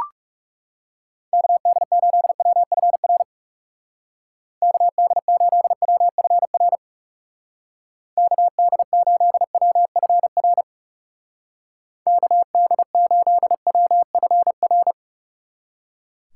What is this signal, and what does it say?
Morse code, heard as a keyed tone, about 35 wpm, 700 Hz. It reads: KD8WFR KD8WFR KD8WFR KD8WFR
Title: Speed Racing - USA 2x3 Call Signs 30wpm